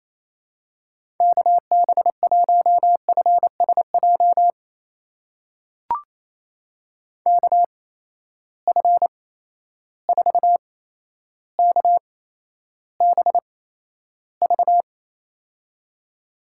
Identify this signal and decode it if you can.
Morse code, heard as a keyed tone, about 27 wpm, spaced as for 12 wpm, 700 Hz. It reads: KB1FSJ KF4KBV
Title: USA Call Signs - 6 Characters - 8x Character Spacing - 28wpm